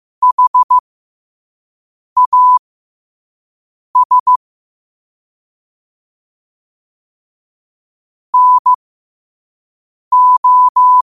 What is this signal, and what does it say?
Morse code, heard as a keyed tone, about 15 wpm, spaced as for 5 wpm, 1000 Hz. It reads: HAS NO
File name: qotdfiles / 5wpm